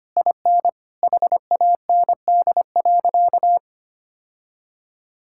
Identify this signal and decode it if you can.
Morse code, heard as a keyed tone, about 25 wpm, 700 Hz. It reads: IN HAND.
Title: Sentences from Top 1000 Words - Review Part 5 - 25wpm